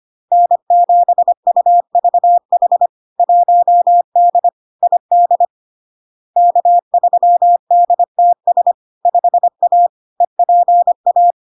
Morse code, heard as a keyed tone, about 25 wpm, 700 Hz. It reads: N7UVH 1D ID K3DTS 5A EPA
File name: LARK-FD-2022-Log-in-CW-25wpm-700-Hz